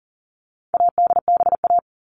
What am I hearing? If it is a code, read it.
Morse code, heard as a keyed tone, about 40 wpm, 700 Hz. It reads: ADBA
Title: All Cut Numbers - 4-Digit 40wpm